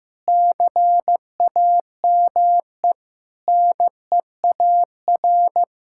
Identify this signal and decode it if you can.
Morse code, heard as a keyed tone, about 15 wpm, 700 Hz. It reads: CAME NEAR
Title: Sentences from Top 1600 Words 15wpm